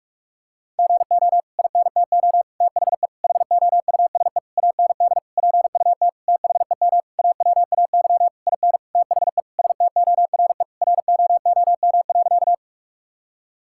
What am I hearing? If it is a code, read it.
Morse code, heard as a keyed tone, about 46 wpm, 700 Hz. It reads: GO INTO THE HOUSE AND PUT THEM AWAY IN THE STORE ROOM.
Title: Sentences from Top 1000 Words - Review Part 5 - 45wpm